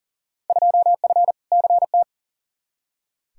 Morse code, heard as a keyed tone, about 37 wpm, 700 Hz. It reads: 2F CT
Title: Field Day Exchanges Encore - 40wpm